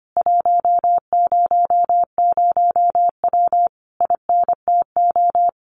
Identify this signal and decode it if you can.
Morse code, heard as a keyed tone, about 25 wpm, 700 Hz. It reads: 100W INTO
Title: Common QSO Phrases Encore 25wpm